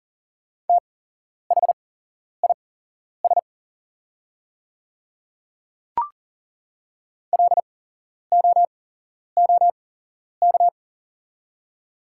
Morse code, heard as a keyed tone, about 40 wpm, spaced as for 11 wpm, 690 Hz. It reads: THIS LOOK
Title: 8x Character Spacing - Top 100 Words - 40wpm